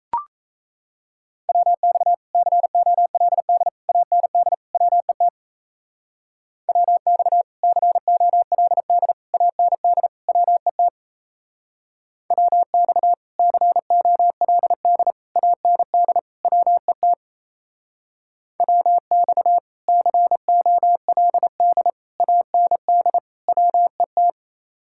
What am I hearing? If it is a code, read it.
Morse code, heard as a keyed tone, about 35 wpm, 700 Hz. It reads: WX COLD AND WET WX COLD AND WET WX COLD AND WET WX COLD AND WET
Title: Speed-Racing Common QSO Phrases - 28wpm